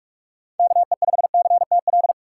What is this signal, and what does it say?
Morse code, heard as a keyed tone, about 44 wpm, 700 Hz. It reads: KE5CTL
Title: USA Call Signs - 6 Characters - 4x Character Spacing - 45wpm